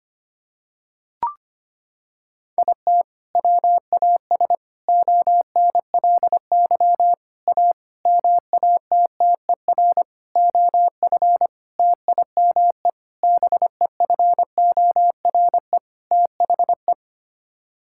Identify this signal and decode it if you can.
Morse code, heard as a keyed tone, about 25 wpm, 700 Hz. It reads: IT WAS ONLY A MATTER OF TIME BEFORE THE
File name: Sets of 9 Words - 25wpm